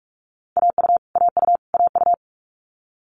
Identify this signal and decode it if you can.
Morse code, heard as a keyed tone, about 46 wpm, 700 Hz. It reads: AU AU AU
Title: Multi-letter Phonemes Repeated 3x 45wpm